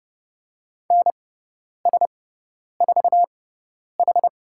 Morse code, heard as a keyed tone, about 30 wpm, spaced as for 10 wpm, 700 Hz. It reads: NS4H
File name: Lesson 17g - Letter u - Call Signs 30wpm